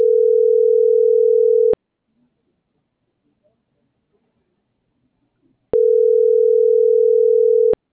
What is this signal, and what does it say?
Ringback tone (North America), 440 + 480 Hz.